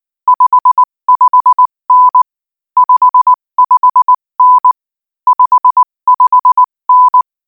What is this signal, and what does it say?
Morse code, heard as a keyed tone, about 19 wpm, 1000 Hz. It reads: 55N 55N 55N
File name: IZ4BEH-13-cm-AS